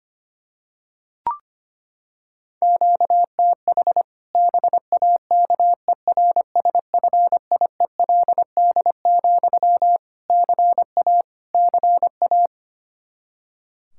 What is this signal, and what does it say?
Morse code, heard as a keyed tone, about 25 wpm, 700 Hz. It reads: QTH BAKERSFIELD, CA CA
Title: Practice QSO - Exchange 1 - Beginning - 25wpm